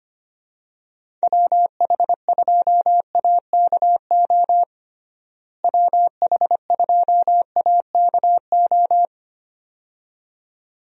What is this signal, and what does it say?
Morse code, heard as a keyed tone, about 25 wpm, 700 Hz. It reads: WH2AKO WH2AKO
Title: USA Call Signs - 6 Characters - Repeated 2x - 25wpm